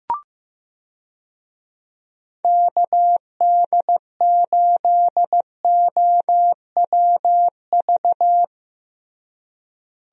Morse code, heard as a keyed tone, about 15 wpm, 700 Hz. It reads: KD8OWV
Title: USA Call Signs - 6 Characters - Repeated 2x - 15wpm